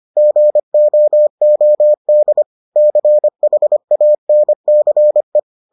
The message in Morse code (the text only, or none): GOOD CHANCE